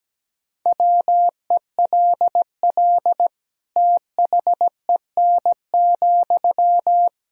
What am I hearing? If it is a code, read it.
Morse code, heard as a keyed tone, about 17 wpm, 700 Hz. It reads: WELL THEN,
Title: Sentences from Top 200 Words and Abbreviations 17wpm